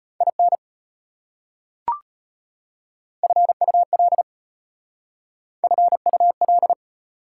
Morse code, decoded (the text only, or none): IN FUL FUL